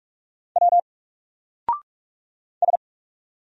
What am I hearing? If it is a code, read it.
Morse code, heard as a keyed tone, about 45 wpm, 700 Hz. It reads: W S
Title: Single Letter-Number 45wpm